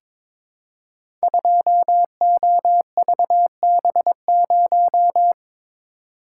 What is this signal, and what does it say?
Morse code, heard as a keyed tone, about 22 wpm, 690 Hz. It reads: 2OVB0
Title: Code Groups - 5 Letter:Numbers - 22wpm